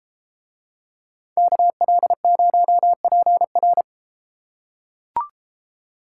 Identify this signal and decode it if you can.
Morse code, heard as a keyed tone, about 33 wpm, 700 Hz. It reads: KL0PR